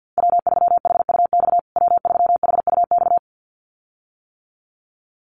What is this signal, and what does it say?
Morse code, heard as a keyed tone, about 50 wpm, 700 Hz. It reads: W3HUX W3HUX
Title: USA Call Signs Repeated 2x 50wpm